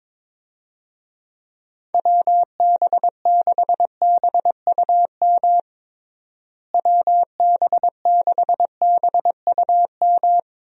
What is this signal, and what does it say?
Morse code, heard as a keyed tone, about 22 wpm, 700 Hz. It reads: WB6BUM WB6BUM
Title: USA Call Signs - 6 Characters - Repeated 2x - 22wpm